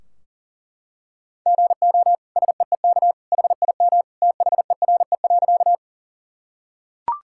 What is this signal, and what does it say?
Morse code, heard as a keyed tone, about 40 wpm, 700 Hz. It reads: GO SEEK HIM THERE.
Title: Sentences from Top 1600 Words 40wpm